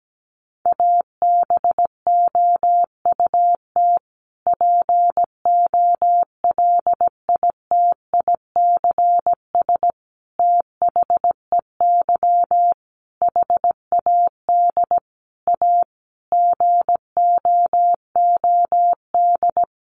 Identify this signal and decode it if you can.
Morse code, heard as a keyed tone, about 17 wpm, 700 Hz. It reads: ABOUT POLITICS THEY HAD A GOOD
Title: Sentences from Top 1000 Words 17wpm